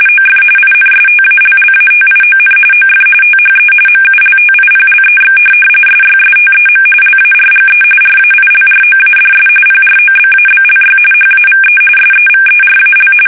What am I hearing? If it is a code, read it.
RTTY, about 45.45 baud. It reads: OF INSTALLATION OF THE MULTIPLEX SYSTEM LARGE RAILROADS PROBABLY HAVE SEVERAL
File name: fsk850